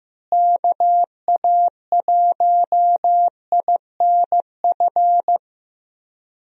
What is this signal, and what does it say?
Morse code, heard as a keyed tone, about 15 wpm, 700 Hz. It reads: KA1INF
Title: USA Call Signs - 6 Characters - Repeated 3x - 15wpm